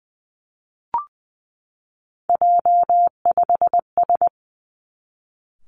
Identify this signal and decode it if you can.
Morse code, heard as a keyed tone, about 20 wpm, 700 Hz. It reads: J5S
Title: Three Challenging Letter-Number Combinations 20wpm